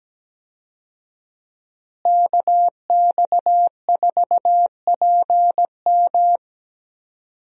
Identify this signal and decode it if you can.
Morse code, heard as a keyed tone, about 17 wpm, 700 Hz. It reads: KX4PM
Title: USA Call Signs - 5 Characters - Repeated 3x - 17wpm